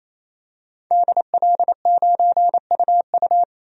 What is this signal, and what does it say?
Morse code, heard as a keyed tone, about 28 wpm, 700 Hz. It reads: DL9UU
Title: DXCC-100 Call Signs - 5 Characters - 28wpm